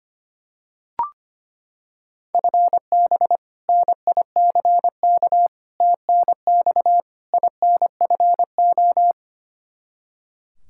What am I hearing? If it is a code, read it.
Morse code, heard as a keyed tone, about 25 wpm, 700 Hz. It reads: FB NICK TNX INFO
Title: Practice QSO - Exchange 3 - More Info - 25wpm